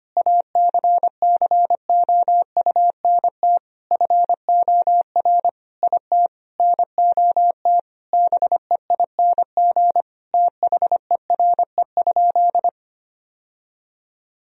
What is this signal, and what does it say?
Morse code, heard as a keyed tone, about 25 wpm, 700 Hz. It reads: ACCOUNT FOR IT NOT BEING THERE?
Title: Sentences from Top 1000 Words - Review Part 3 - 25wpm